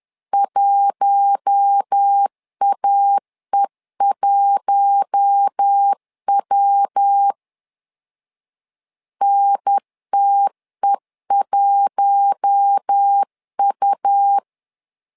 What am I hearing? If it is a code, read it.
Morse code, heard as a keyed tone, about 10 wpm, 800 Hz. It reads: 1AE1W NTE1U
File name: LEZIONI / 4 Q1 10wpm 2448